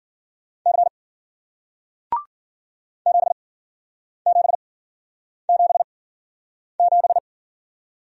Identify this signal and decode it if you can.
Morse code, heard as a keyed tone, about 50 wpm, spaced as for 9 wpm, 700 Hz. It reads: C 7777
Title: Speed-Racing Single Letter-Number 40wpm